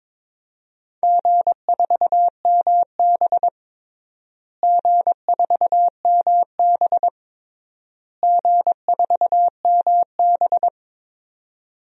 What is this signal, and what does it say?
Morse code, heard as a keyed tone, about 22 wpm, 700 Hz. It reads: G4MB G4MB G4MB
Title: DXCC-100 Call Signs - 4 Characters - Repeated 3x - 22wpm